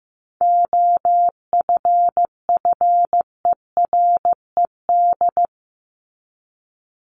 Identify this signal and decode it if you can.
Morse code, heard as a keyed tone, about 15 wpm, 700 Hz. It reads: OFFERED
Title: ED Suffix - Top 100 Words with Base Words 15wpm